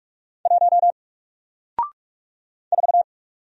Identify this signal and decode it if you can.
Morse code, heard as a keyed tone, about 44 wpm, 700 Hz. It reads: 1 4
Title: Single Letter-Number Alt 45wpm